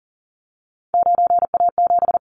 Morse code, heard as a keyed tone, about 40 wpm, 700 Hz. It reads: 9A7
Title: Callsign Prefixes for Top 100 DXCC - 40wpm